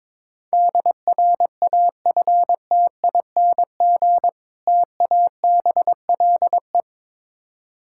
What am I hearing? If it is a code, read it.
Morse code, heard as a keyed tone, about 22 wpm, 690 Hz. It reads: DRAFTING TABLE
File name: Difficult Words with Hints - 22wpm